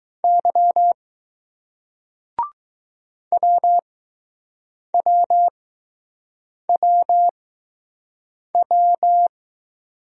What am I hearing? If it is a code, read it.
Morse code, heard as a keyed tone, about 21 wpm, spaced as for 6 wpm, 700 Hz. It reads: Y WWWW